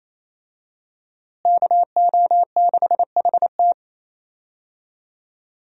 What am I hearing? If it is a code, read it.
Morse code, heard as a keyed tone, about 28 wpm, 700 Hz. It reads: KO6HT
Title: USA Call Signs - 5 Characters - Repeated 2x - 28wpm